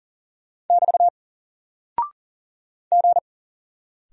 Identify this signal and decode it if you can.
Morse code, heard as a keyed tone, about 40 wpm, 700 Hz. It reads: <BT> G
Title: Single Character (Letter:Number:Punctuation:Prosign) 40wpm